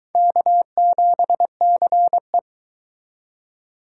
Morse code, heard as a keyed tone, about 22 wpm, 700 Hz. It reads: K7CE